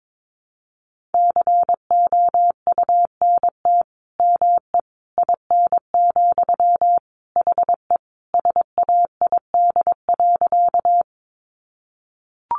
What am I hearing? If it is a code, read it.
Morse code, heard as a keyed tone, about 22 wpm, 700 Hz. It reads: COUNT ME IN, HE SAID.
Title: Sentences from Top 1500 Words 22wpm